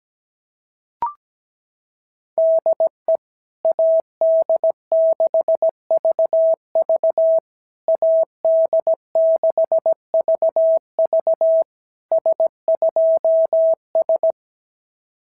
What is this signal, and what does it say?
Morse code, heard as a keyed tone, about 17 wpm, 650 Hz. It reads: DE AD6VV AD6VV S2S
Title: SOTA QSO Practice 17wpm